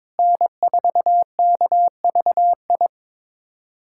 Morse code, heard as a keyed tone, about 22 wpm, 700 Hz. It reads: N4KVI
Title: USA Callsign DE Callsign - 5 Characters - 22wpm